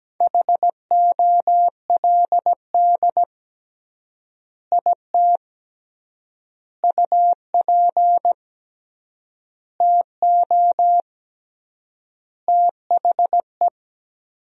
Morse code, heard as a keyed tone, about 17 wpm, 700 Hz. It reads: HOLD IT UP TO THE
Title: 3x Word-Spacing - Sentences from Top 400 Words - 17wpm